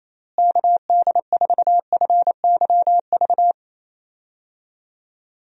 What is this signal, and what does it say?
Morse code, heard as a keyed tone, about 27 wpm, 700 Hz. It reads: KD4FYV
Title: USA Call Signs - 6 Characters - Repeated 3x - 28wpm